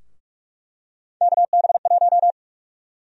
Morse code, heard as a keyed tone, about 46 wpm, 700 Hz. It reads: KB1
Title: USA Call Sign Prefixes - 45wpm